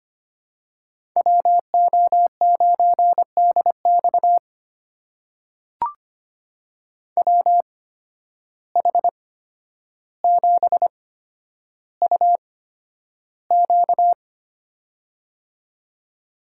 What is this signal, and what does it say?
Morse code, heard as a keyed tone, about 25 wpm, spaced as for 11 wpm, 700 Hz. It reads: WO9DX WH7UQ
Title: USA Call Signs - 5 Characters - 8x Character Spacing - 25wpm